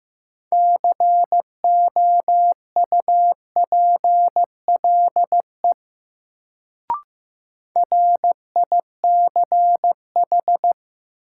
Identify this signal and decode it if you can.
Morse code, heard as a keyed tone, about 15 wpm, 700 Hz. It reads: COUPLE RICH
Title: Sentences from Top 1100 Words - New Word Drill - 15wpm